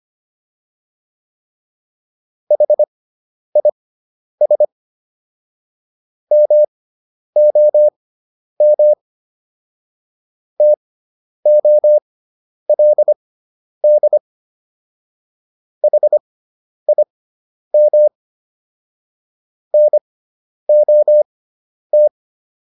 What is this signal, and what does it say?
Morse code, heard as a keyed tone, about 25 wpm, spaced as for 10 wpm, 600 Hz. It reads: HIS MOM TOLD HIM NOT